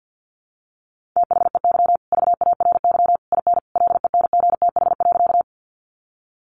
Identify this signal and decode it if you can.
Morse code, heard as a keyed tone, about 50 wpm, 700 Hz. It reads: THEY VARY IN LENGTH.
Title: Sentences from Top 1200 Words 50wpm